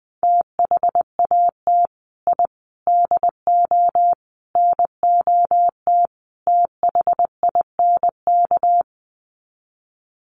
Morse code, heard as a keyed tone, about 20 wpm, 700 Hz. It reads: THAT I DO NOT THINK
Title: Sets of 5 Words Encore 20wpm